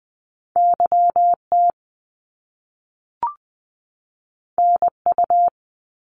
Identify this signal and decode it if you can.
Morse code, heard as a keyed tone, about 20 wpm, 700 Hz. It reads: YT NU
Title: Canadian Provinces and Territories 20wpm